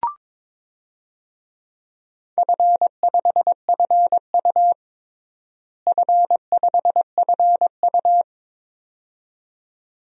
Morse code, heard as a keyed tone, about 22 wpm, 700 Hz. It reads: F5FU F5FU
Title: DXCC-100 Call Signs - 4 Characters - Repeated 2x - 22wpm